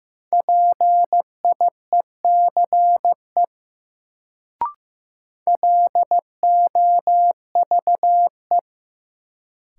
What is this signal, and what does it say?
Morse code, heard as a keyed tone, about 16 wpm, 700 Hz. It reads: PIECE LOVE
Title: Sentences from Top 400 Words - New Word Drill - 15wpm